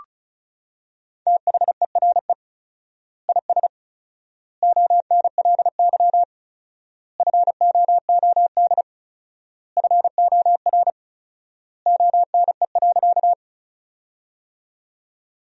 Morse code, heard as a keyed tone, about 34 wpm, 700 Hz. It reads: THERE IS ONLY FOOD FOR ONE.
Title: 4x Word-Spacing - Sentences from Top 300 Words - 35wpm